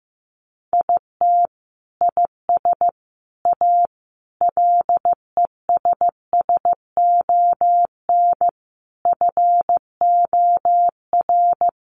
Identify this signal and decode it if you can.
Morse code, heard as a keyed tone, about 15 wpm, 700 Hz. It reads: IT IS A LESSON FOR
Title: Sentences from Top 1200 Words 15wpm